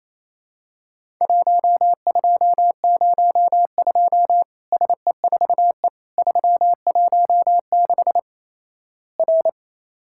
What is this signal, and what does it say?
Morse code, heard as a keyed tone, about 27 wpm, 700 Hz. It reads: 1202 SE4E 316 R
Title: Worked All Europe DX Contest - QTC Exchange - 28wpm